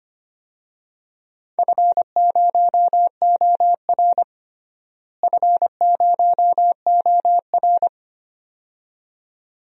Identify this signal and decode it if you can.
Morse code, heard as a keyed tone, about 25 wpm, 700 Hz. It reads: F0OR F0OR
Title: DXCC-100 Call Signs - 4 Characters - Repeated 2x - 25wpm